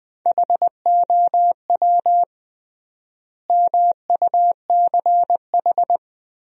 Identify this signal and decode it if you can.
Morse code, heard as a keyed tone, about 21 wpm, 700 Hz. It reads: HOW MUCH
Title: 3x Word-Spacing - Sentences from Top 300 Words - 20wpm